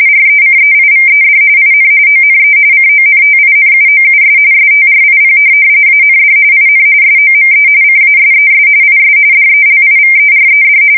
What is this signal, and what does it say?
RTTY, about 45.45 baud. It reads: RECEIVED IN ONE CONTINUOUS STRIP AND IN SOME CASES IS DELIVERED
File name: fsk170 / printing-telegraph